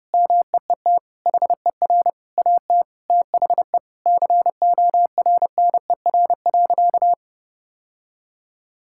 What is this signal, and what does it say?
Morse code, heard as a keyed tone, about 32 wpm, 700 Hz. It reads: MEET HER AT THE CORNER.
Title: Sentences from Top 1600 Words 30wpm